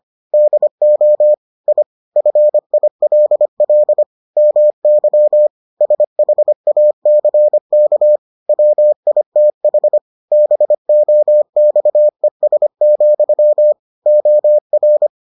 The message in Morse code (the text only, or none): DO I FILL MY SHACK WITH BOXES, OR